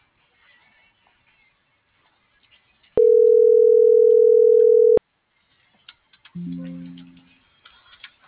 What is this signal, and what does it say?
Ringback tone (North America), 440 + 480 Hz.